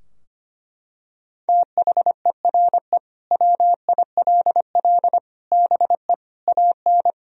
Morse code, heard as a keyed tone, about 25 wpm, 700 Hz. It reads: THERE WILL BE AN
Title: Sentences from Top 1000 Words - Review Part 4 - 25wpm